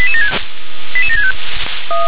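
Five-tone selective call (ZVEI), heard as quick runs of five tones, tones of about 70 ms. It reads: 88765
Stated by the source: ZVEI five-tone selective call received by SDR.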